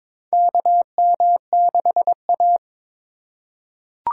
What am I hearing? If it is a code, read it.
Morse code, heard as a keyed tone, about 21 wpm, 700 Hz. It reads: KM6A
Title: USA Call Signs - 4 Characters - 2x Character Spacing - 22wpm